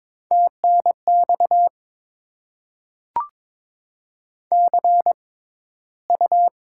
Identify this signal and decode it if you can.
Morse code, heard as a keyed tone, about 22 wpm, spaced as for 14 wpm, 700 Hz. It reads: TNX CU
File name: Top 20 QSO Elements - 6x Character Spacing - 22wpm